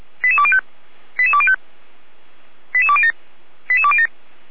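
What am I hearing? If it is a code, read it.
Five-tone selective call (ZVEI), heard as quick runs of five tones, tones of about 70 ms. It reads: 88205 88205 88207 88207
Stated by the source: ZVEI five-tone selective call received on SDR.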